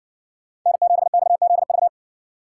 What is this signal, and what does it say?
Morse code, heard as a keyed tone, about 55 wpm, 700 Hz. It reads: N7XZV